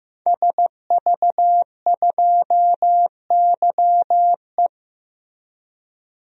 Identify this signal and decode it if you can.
Morse code, heard as a keyed tone, about 16 wpm, 700 Hz. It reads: SV2YE
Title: DXCC-100 Call Signs - 5 Characters - Repeated 3x - 15wpm